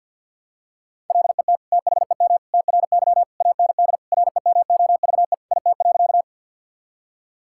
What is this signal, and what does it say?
Morse code, heard as a keyed tone, about 50 wpm, 700 Hz. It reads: LET THEM TRY AND REMOVE IT.
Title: Sentences from Top 1000 Words - Review Part 5 - 50wpm